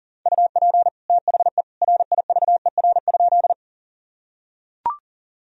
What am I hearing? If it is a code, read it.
Morse code, heard as a keyed tone, about 40 wpm, 700 Hz. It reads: UP THE RIVER?
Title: Sentences from Top 1000 Words - Review Part 1 - 40wpm